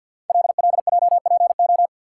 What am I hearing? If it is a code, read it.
Morse code, heard as a keyed tone, about 53 wpm, 700 Hz. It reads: LLJPO
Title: Code Groups - 5 Letters - 50wpm